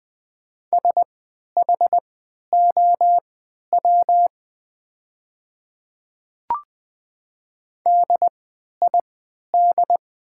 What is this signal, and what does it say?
Morse code, heard as a keyed tone, about 20 wpm, spaced as for 11 wpm, 690 Hz. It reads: SHOW DID
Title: 3x Character Spacing - Top 300 Words - 20wpm